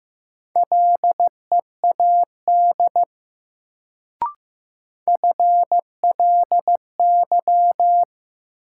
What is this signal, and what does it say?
Morse code, heard as a keyed tone, about 15 wpm, 700 Hz. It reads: LEAD FLY
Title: Sentences from Top 500 Words - New Word Drill - 15wpm